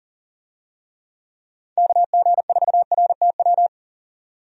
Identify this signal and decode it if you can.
Morse code, heard as a keyed tone, about 40 wpm, 700 Hz. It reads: KG4RTW
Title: USA Call Signs - 6 Characters - 40wpm